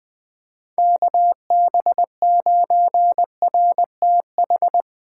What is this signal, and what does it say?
Morse code, heard as a keyed tone, about 23 wpm, 700 Hz. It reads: KB9RTH
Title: Speed Racing - USA 2x3 Call Signs 20wpm